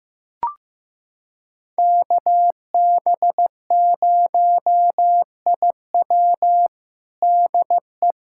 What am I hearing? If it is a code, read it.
Morse code, heard as a keyed tone, about 15 wpm, 700 Hz. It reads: KB0IW DE
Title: USA Callsign DE Callsign 15wpm